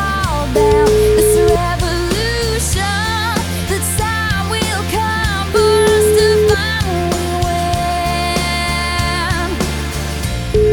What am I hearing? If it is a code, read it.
Ringback tone (Europe, 425 Hz), with noise.